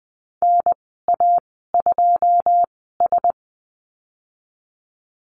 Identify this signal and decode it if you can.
Morse code, heard as a keyed tone, about 20 wpm, spaced as for 15 wpm, 700 Hz. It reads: NA2S
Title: 082 Lesson 14g - Number 2 - Call Signs 20wpm